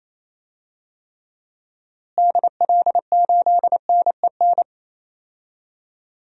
Morse code, heard as a keyed tone, about 28 wpm, 700 Hz. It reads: DL8NEN
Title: DXCC-100 Call Signs - 6 Characters - Repeated 3x - 28wpm